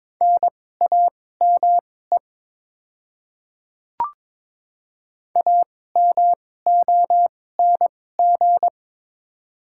Morse code, heard as a keyed tone, about 21 wpm, spaced as for 16 wpm, 700 Hz. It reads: NAME AMONG
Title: 2x Character Spacing - Top 500 Words - 22wpm